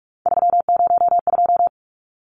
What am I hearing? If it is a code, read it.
Morse code, heard as a keyed tone, about 44 wpm, 700 Hz. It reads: 302
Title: Numbers - 3-Digit 45wpm